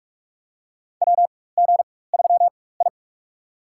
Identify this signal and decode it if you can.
Morse code, heard as a keyed tone, about 41 wpm, spaced as for 20 wpm, 700 Hz. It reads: WG3I
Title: USA Call Signs - 4x Character Spacing - 45wpm